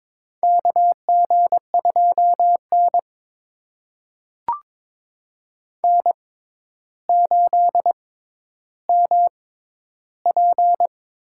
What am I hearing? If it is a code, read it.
Morse code, heard as a keyed tone, about 22 wpm, spaced as for 11 wpm, 700 Hz. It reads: KG2N N8MP
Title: USA Call Signs - 4 Characters - 6x Character Spacing - 22wpm